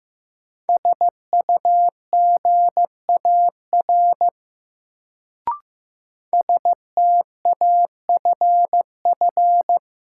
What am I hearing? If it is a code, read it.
Morse code, heard as a keyed tone, about 15 wpm, 700 Hz. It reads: SUGAR STAFF
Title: Sentences from Top 1200 Words - New Word Drill - 15wpm